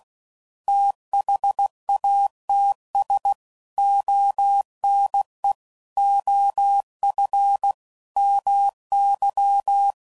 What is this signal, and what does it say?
Morse code, heard as a keyed tone, about 16 wpm, 800 Hz. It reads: THATS ONE OF MY
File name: CW Code Groups @15WPM/QSOs/WD0FBT Spaced